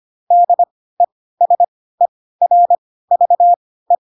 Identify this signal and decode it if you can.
Morse code, heard as a keyed tone, about 25 wpm, spaced as for 16 wpm, 700 Hz. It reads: DESERVE
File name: Pinocchio / Pinocchio ch3 10 25 ik1hge